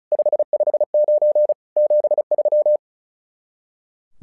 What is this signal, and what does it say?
Morse code, heard as a keyed tone, about 36 wpm, 600 Hz. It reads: 559 73
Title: SOTA QSO Practice 35wpm